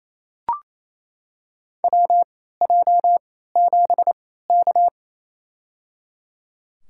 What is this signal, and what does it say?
Morse code, heard as a keyed tone, about 28 wpm, spaced as for 16 wpm, 700 Hz. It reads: WJ7K
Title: USA Call Signs - 3x Character Spacing - 28wpm